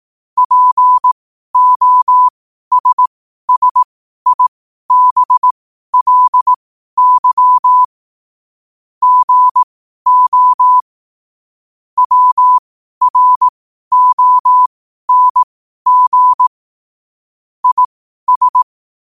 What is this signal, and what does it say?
Morse code, heard as a keyed tone, about 18 wpm, spaced as for 12 wpm, 1000 Hz. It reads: POSSIBLY GO WRONG IS
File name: qotdfiles / 12wpm